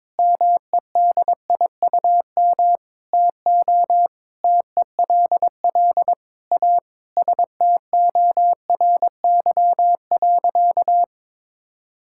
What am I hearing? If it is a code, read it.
Morse code, heard as a keyed tone, about 22 wpm, 700 Hz. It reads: MEDIUM TO TELL A STORY.
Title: Sentences from Top 1000 Words - Review Part 6 - 22wpm